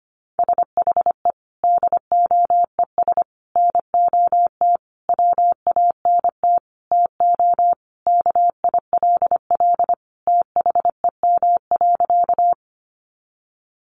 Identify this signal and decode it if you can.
Morse code, heard as a keyed tone, about 25 wpm, 700 Hz. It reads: SHE DOES NOT WANT TO KILL THEM.
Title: Additional Easy Phrases 25wpm